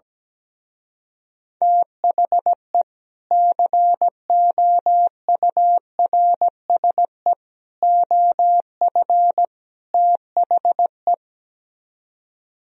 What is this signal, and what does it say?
Morse code, heard as a keyed tone, about 17 wpm, 700 Hz. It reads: THE COURSE OF THE
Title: Speed-Racing - Sets of 4 Words 17wpm